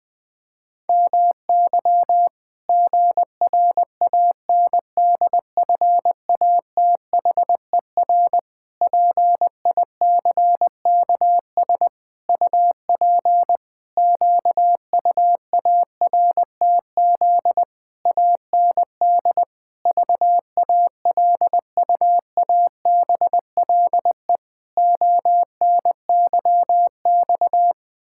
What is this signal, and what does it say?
Morse code, heard as a keyed tone, about 20 wpm, 700 Hz. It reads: MY GRANDFATHER PICKS UP QUARTZ AND VALUABLE ONYX
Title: Pangram Challenge v2 - 20wpm